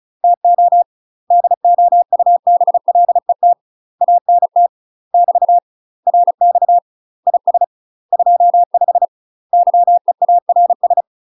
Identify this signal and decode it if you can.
Morse code, heard as a keyed tone, about 36 wpm, 700 Hz. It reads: TO DOUBLET ANT <BT> RX IS 25 YEARS
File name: UK MASTER QSOs TRAINING/35WPM